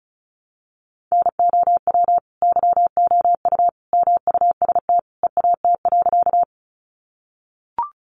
Morse code, heard as a keyed tone, about 34 wpm, 700 Hz. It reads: NOW YOU MUST EAT.
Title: Sentences from Top 300 Words 35wpm